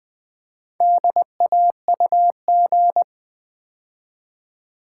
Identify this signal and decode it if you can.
Morse code, heard as a keyed tone, about 21 wpm, 700 Hz. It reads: DAUG
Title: Build A Word - 2 Letters At A Time - 8-Letter Words - 20wpm